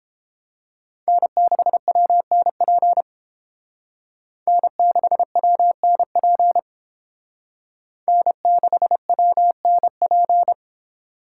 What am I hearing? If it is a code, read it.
Morse code, heard as a keyed tone, about 28 wpm, 700 Hz. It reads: N6WNP N6WNP N6WNP